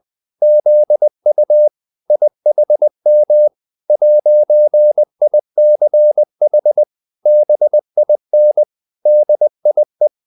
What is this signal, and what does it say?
Morse code, heard as a keyed tone, about 20 wpm, 600 Hz. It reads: ZU IHM 'ICH BIN DIE